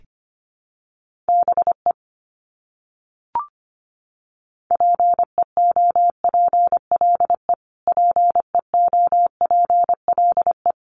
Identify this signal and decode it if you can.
Morse code, heard as a keyed tone, about 25 wpm, 700 Hz. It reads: BE PEOPLE PEOPLE
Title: Top 300 Words Repeated 25wpm